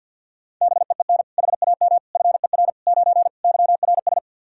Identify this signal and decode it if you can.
Morse code, heard as a keyed tone, about 45 wpm, 700 Hz. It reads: BEEN HAM FER 9 YRS
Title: Practice QSO - Exchange 3 - More Info - 50wpm